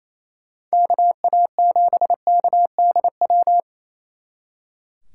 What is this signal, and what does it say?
Morse code, heard as a keyed tone, about 28 wpm, 700 Hz. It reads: KA7KDW
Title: Speed Racing - USA 2x3 Call Signs 28wpm